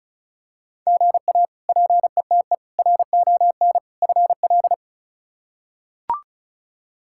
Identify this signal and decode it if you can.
Morse code, heard as a keyed tone, about 34 wpm, 700 Hz. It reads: GA PETE RON FL
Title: K1USN SST QSO Practice - 35wpm